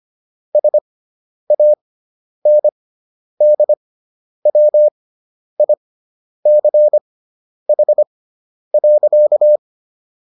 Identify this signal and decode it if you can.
Morse code, heard as a keyed tone, about 25 wpm, spaced as for 10 wpm, 600 Hz. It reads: SANDWICH.